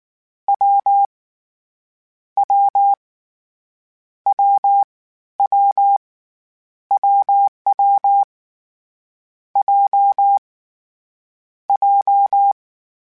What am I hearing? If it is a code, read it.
Morse code, heard as a keyed tone, about 19 wpm, spaced as for 14 wpm, 800 Hz. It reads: W W WW WW J J
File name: Difficult Combos / Difficult Combos-mp3s / AWJ ShortRun@18